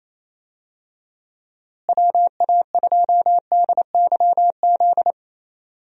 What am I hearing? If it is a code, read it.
Morse code, heard as a keyed tone, about 28 wpm, 700 Hz. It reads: WA2DYZ
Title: USA Call Signs - 6 Characters - Repeated 3x - 28wpm